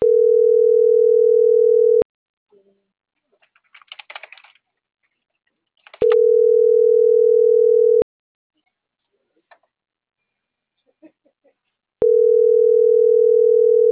Ringback tone (North America), 440 + 480 Hz.